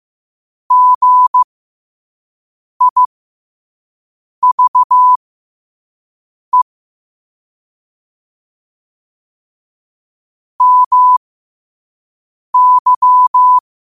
Morse code, heard as a keyed tone, about 15 wpm, spaced as for 5 wpm, 1000 Hz. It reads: GIVE MY
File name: qotdfiles / 5wpm